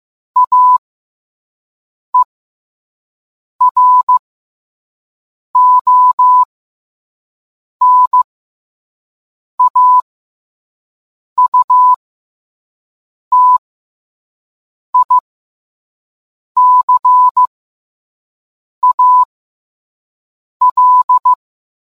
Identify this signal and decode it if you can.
Morse code, heard as a keyed tone, about 15 wpm, spaced as for 5 wpm, 1000 Hz. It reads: AERONAUTICAL